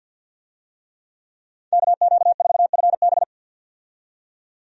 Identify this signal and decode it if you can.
Morse code, heard as a keyed tone, about 52 wpm, 700 Hz. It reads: KQ4FB
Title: USA Call Signs - 5 Characters - 50wpm